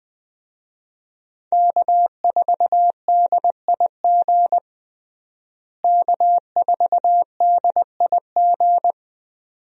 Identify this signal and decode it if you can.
Morse code, heard as a keyed tone, about 20 wpm, 700 Hz. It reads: K4DIG K4DIG
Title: USA Call Signs - 5 Characters - Repeated 2x - 20wpm